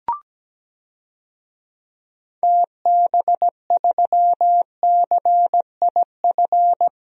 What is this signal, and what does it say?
Morse code, heard as a keyed tone, about 17 wpm, 700 Hz. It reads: TB3CIF
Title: DXCC-100 Call Signs - 6 Characters - Repeated 2x - 17wpm